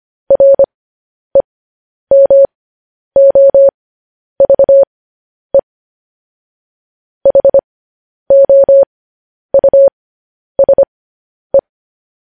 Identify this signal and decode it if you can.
Morse code, heard as a keyed tone, about 25 wpm, spaced as for 10 wpm, 550 Hz. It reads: REMOVE HOUSE